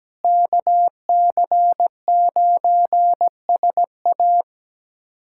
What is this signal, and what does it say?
Morse code, heard as a keyed tone, about 17 wpm, 700 Hz. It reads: KC9SA
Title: USA Callsign DE Callsign - 5 Characters - 17wpm